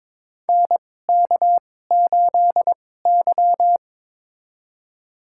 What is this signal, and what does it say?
Morse code, heard as a keyed tone, about 22 wpm, spaced as for 16 wpm, 700 Hz. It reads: NK8Y
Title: USA Call Signs - 4 Characters - 2x Character Spacing - 22wpm